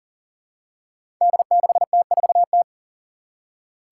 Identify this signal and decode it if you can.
Morse code, heard as a keyed tone, about 40 wpm, 700 Hz. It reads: D6T4T
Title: Five Challenging Letter-Number Combinations 40wpm